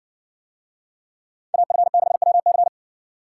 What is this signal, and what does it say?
Morse code, heard as a keyed tone, about 57 wpm, 700 Hz. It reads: AF6WC